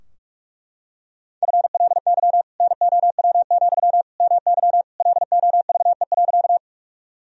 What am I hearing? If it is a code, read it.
Morse code, heard as a keyed tone, about 41 wpm, 700 Hz. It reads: FLY NOW, MY LOVE.
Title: Sentences from Top 1000 Words - Review Part 2 - 45wpm